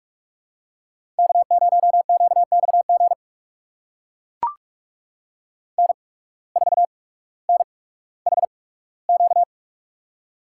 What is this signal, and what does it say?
Morse code, heard as a keyed tone, about 44 wpm, spaced as for 19 wpm, 700 Hz. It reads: K0QXG N4NHQ
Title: USA Call Signs - 8x Character Spacing - 45wpm